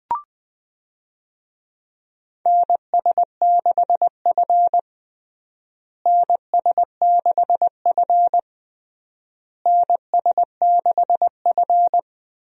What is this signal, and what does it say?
Morse code, heard as a keyed tone, about 20 wpm, 700 Hz. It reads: NS6F NS6F NS6F
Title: USA Call Signs - 4 Characters - Repeated 3x - 20wpm